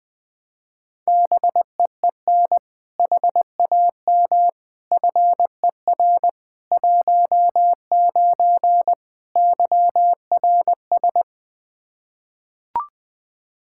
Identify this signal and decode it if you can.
Morse code, heard as a keyed tone, about 20 wpm, 700 Hz. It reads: BEEN HAM FER 19 YRS
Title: Practice QSO - Exchange 3 - More Info - 20wpm